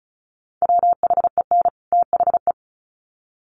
Morse code, heard as a keyed tone, about 34 wpm, 700 Hz. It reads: WHEN THE
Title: Sets of 2 Words 35wpm